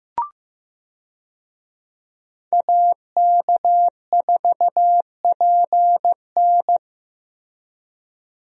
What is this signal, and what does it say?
Morse code, heard as a keyed tone, about 15 wpm, 700 Hz. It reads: AK4PN
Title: USA Call Signs - 5 Characters - Repeated 2x - 15wpm